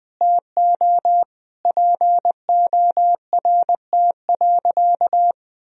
Morse code, heard as a keyed tone, about 21 wpm, 700 Hz. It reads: TO PORT.
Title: Sentences from Top 1000 Words - Review Part 1 - 20wpm